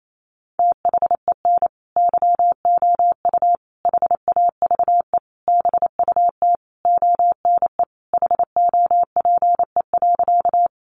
Morse code, heard as a keyed tone, about 28 wpm, 700 Hz. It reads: THEN YOU HAVE BUT ONE HOPE.
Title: Sentences from Top 1000 Words 28wpm